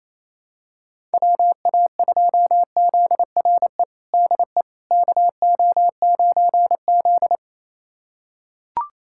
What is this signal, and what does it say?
Morse code, heard as a keyed tone, about 28 wpm, 700 Hz. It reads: WA2ZRE DE KO9Z
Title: USA Callsign DE Callsign 28wpm